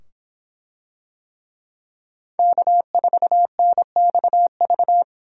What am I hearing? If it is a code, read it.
Morse code, heard as a keyed tone, about 24 wpm, 700 Hz. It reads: K4NXV